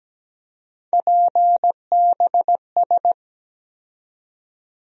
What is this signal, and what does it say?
Morse code, heard as a keyed tone, about 17 wpm, 700 Hz. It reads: PBS
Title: USA Call Sign Postfixes - 3 Letters 17wpm